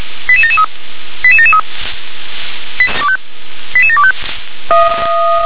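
Five-tone selective call (ZVEI), heard as quick runs of five tones, tones of about 70 ms. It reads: 88773 88773 88725
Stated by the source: ZVEI five-tone call 88556, received by SDR.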